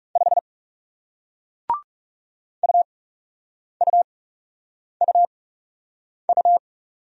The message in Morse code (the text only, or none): 5 UUUU